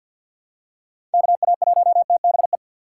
Morse code, heard as a keyed tone, about 48 wpm, 690 Hz. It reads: KA1TBE
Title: USA Call Signs - 6 Characters - 8x Character Spacing - 50wpm